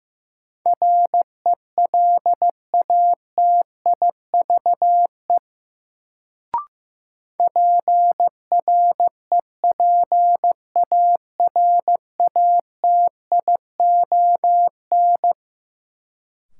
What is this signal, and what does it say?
Morse code, heard as a keyed tone, about 16 wpm, 700 Hz. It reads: RELATIVE PREPARATION
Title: Sentences from Top 1400 Words - New Word Drill - 15wpm